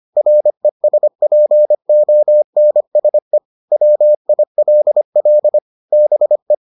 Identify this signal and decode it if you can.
Morse code, heard as a keyed tone, about 25 wpm, 600 Hz. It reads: RESPONSE WILL BE